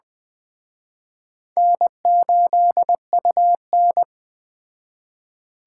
Morse code, heard as a keyed tone, about 20 wpm, 700 Hz. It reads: N8UN